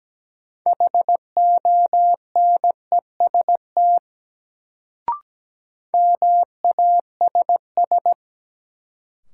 Morse code, heard as a keyed tone, about 17 wpm, 700 Hz. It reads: HONEST MASS
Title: Sets of 5 Words Encore - New Word Drill - 17wpm